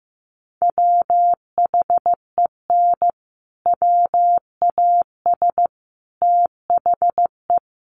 Morse code, heard as a keyed tone, about 15 wpm, 700 Hz. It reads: WHEN WAS THE
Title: Sets of 5 Words Encore 15wpm